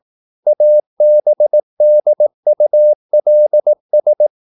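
Morse code, heard as a keyed tone, about 18 wpm, 600 Hz. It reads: ABDULS
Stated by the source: Morse practice at 18 wpm, English text, from CW med Gnister.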